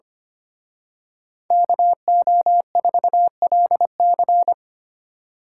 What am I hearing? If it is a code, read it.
Morse code, heard as a keyed tone, about 29 wpm, 700 Hz. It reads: KO4LC
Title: Speed Racing - USA 2x2 Call Signs 25wpm